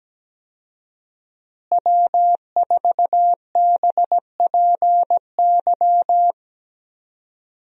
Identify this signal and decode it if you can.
Morse code, heard as a keyed tone, about 17 wpm, 700 Hz. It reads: W4BPY
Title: USA Call Signs - 5 Characters - Repeated 3x - 17wpm